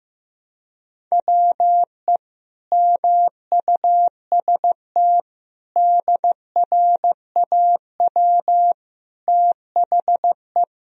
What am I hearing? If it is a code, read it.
Morse code, heard as a keyed tone, about 15 wpm, 700 Hz. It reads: WE MUST DRAW THE
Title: Sentences from Top 1000 Words - Review Part 1 - 15wpm